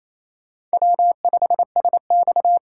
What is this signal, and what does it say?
Morse code, heard as a keyed tone, about 36 wpm, 700 Hz. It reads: W5SX